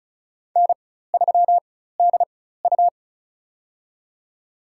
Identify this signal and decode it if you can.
Morse code, heard as a keyed tone, about 34 wpm, spaced as for 16 wpm, 700 Hz. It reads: N3DU
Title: USA Call Signs - 4 Characters - 4x Character Spacing - 35wpm